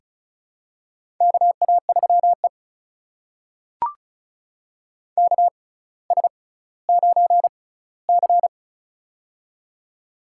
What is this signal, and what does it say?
Morse code, heard as a keyed tone, about 36 wpm, spaced as for 18 wpm, 700 Hz. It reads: KA3E KS9C
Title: USA Call Signs - 4 Characters - 6x Character Spacing - 35wpm